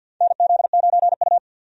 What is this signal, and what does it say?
Morse code, heard as a keyed tone, about 50 wpm, 700 Hz. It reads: NZ9U
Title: USA Call Signs - 4 Characters - 50wpm